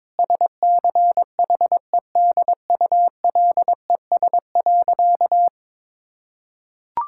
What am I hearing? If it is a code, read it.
Morse code, heard as a keyed tone, about 23 wpm, 700 Hz. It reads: SCHEDULES.
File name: Difficult Sentences with Hints - 22wpm